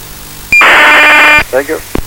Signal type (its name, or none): ACARS aircraft data